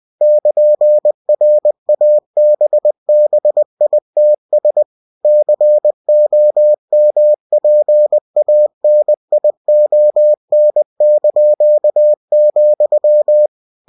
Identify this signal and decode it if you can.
Morse code, heard as a keyed tone, about 20 wpm, 600 Hz. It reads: <KN>RABBITS COMPANION),
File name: cw-books / winnie-the-pooh / winnie-20wpm-page-0099